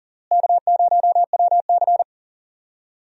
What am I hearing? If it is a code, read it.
Morse code, heard as a keyed tone, about 40 wpm, 700 Hz. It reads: K0WC
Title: USA Call Signs - 4 Characters - Repeated 3x - 40wpm